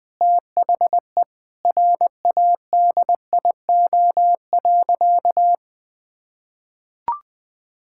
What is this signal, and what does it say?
Morse code, heard as a keyed tone, about 20 wpm, 700 Hz. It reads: THE RADIO.
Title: Sentences with 5-Word Phrases - 20wpm